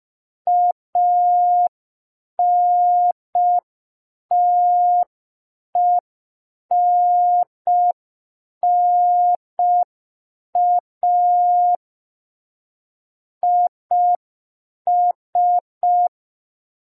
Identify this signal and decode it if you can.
Morse code, heard as a keyed tone, about 5 wpm, 710 Hz. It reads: ANTENNA IS
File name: g3zpd / Morse code P2